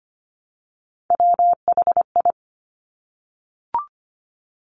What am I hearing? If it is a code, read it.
Morse code, heard as a keyed tone, about 25 wpm, 700 Hz. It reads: WHI
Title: Top Three Letter Combinations in Words 25wpm